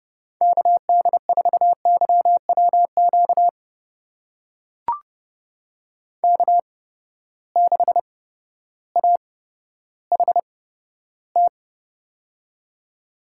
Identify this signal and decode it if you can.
Morse code, heard as a keyed tone, about 30 wpm, spaced as for 14 wpm, 700 Hz. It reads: KD4YWQ K6AHT
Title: USA Call Signs - 8x Character Spacing - 30wpm